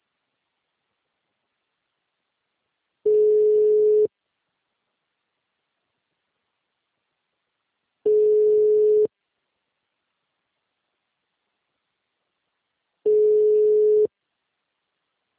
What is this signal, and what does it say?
Ringback tone (Europe, 425 Hz).